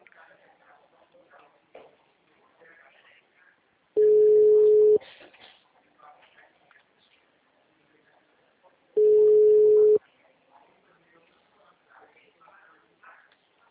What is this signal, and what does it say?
Ringback tone (Europe, 425 Hz), with speech.